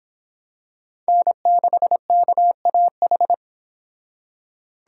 Morse code, heard as a keyed tone, about 25 wpm, 700 Hz. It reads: N6KAH